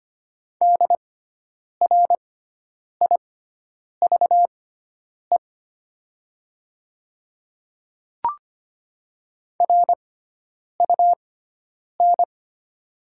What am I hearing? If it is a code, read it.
Morse code, heard as a keyed tone, about 25 wpm, spaced as for 9 wpm, 700 Hz. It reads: DRIVE RUN
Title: 6x Character Spacing - Top 500 Words - 25wpm